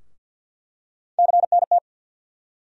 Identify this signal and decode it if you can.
Morse code, heard as a keyed tone, about 50 wpm, 700 Hz. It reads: CNT
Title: USA Call Sign Postfixes - 3 Letters 50wpm